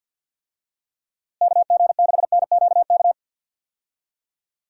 Morse code, heard as a keyed tone, about 51 wpm, 700 Hz. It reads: KG6NQK
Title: USA Call Signs - 6 Characters - Repeated 2x - 50wpm